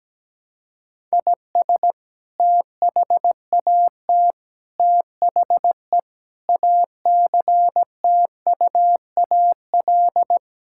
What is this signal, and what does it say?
Morse code, heard as a keyed tone, about 17 wpm, 700 Hz. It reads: IS THAT THE ACTUAL
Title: Sentences from Top 1000 Words - Review Part 5 - 17wpm